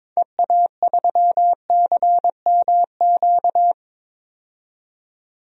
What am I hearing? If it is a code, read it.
Morse code, heard as a keyed tone, about 21 wpm, 700 Hz. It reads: EA3CMQ
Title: DXCC-100 Call Signs - 6 Characters - 22wpm